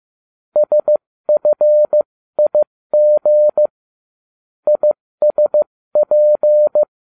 Morse code, heard as a keyed tone, about 15 wpm, spaced as for 13 wpm, 600 Hz. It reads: SFIG ISP